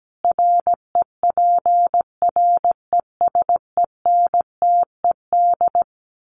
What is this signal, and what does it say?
Morse code, heard as a keyed tone, about 17 wpm, 700 Hz. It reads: REPRESENTED
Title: Sets of 5 Words 17wpm